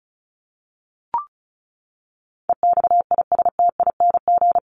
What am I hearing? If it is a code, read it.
Morse code, heard as a keyed tone, about 34 wpm, 700 Hz. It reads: EXISTING
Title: ING Suffix - Top 100 Words with Base Words 35wpm